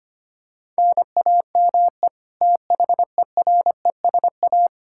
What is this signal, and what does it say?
Morse code, heard as a keyed tone, about 25 wpm, 700 Hz. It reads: NAME THERESA
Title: Practice QSO - Exchange 1 - Beginning - 25wpm